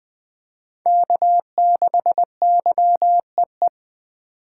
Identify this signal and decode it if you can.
Morse code, heard as a keyed tone, about 20 wpm, 700 Hz. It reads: K6YEE
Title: USA Call Signs - 5 Characters - 3x Character Spacing - 20wpm